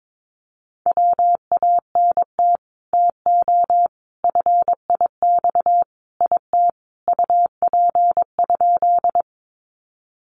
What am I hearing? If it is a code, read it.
Morse code, heard as a keyed tone, about 22 wpm, 700 Hz. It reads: WANT TO FIX IT UP?
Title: Sentences from Top 1400 Words 22wpm